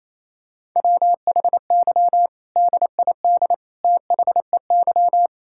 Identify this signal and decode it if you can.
Morse code, heard as a keyed tone, about 28 wpm, 700 Hz. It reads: WHY DID THEY
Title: Sentences from Top 1000 Words - Review Part 6 - 28wpm